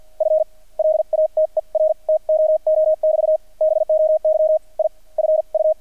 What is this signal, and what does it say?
Morse code, heard as a keyed tone, about 48 wpm, spaced as for 35 wpm, 620 Hz. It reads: FLATER TOO<BT> BOY IF U
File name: archived / CRSnet